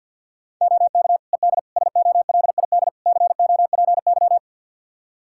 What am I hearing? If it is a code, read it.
Morse code, heard as a keyed tone, about 47 wpm, 700 Hz. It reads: OK ED SOLID COPY
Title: K1USN SST QSO Practice 50wpm